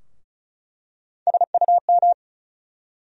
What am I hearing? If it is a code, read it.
Morse code, heard as a keyed tone, about 35 wpm, 700 Hz. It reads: SUM
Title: USA Call Sign Postfixes - 3 Letters 35wpm